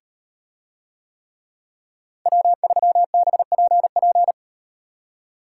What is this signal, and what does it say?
Morse code, heard as a keyed tone, about 38 wpm, 700 Hz. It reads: W3BPP